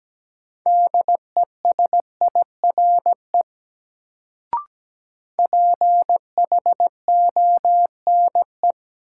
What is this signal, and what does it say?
Morse code, heard as a keyed tone, about 17 wpm, 700 Hz. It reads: DESIRE PHONE
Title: Sentences from Top 1100 Words - New Word Drill Alt - 17wpm